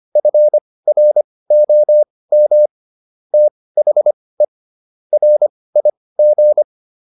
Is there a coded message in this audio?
Yes: Morse code from 0.2 to 6.6 s, heard as a keyed tone, about 25 wpm, spaced as for 18 wpm, 600 Hz. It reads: FROM THE RIG